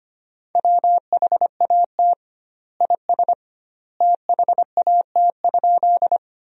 Morse code, heard as a keyed tone, about 25 wpm, 700 Hz. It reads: WHAT IS THAT?
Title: 2x Word-Spacing - Sentences from Top 100 Words - 25wpm